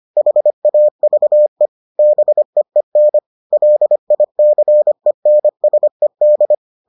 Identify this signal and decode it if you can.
Morse code, heard as a keyed tone, about 25 wpm, 600 Hz. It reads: HAVE BEEN LICENSED